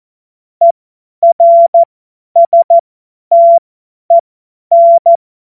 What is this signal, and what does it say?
Morse code, heard as a keyed tone, about 14 wpm, spaced as for 10 wpm, 680 Hz. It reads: ERSTEN
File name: Morsekurs / Witze / Witz 27. Januar 2026 50 F70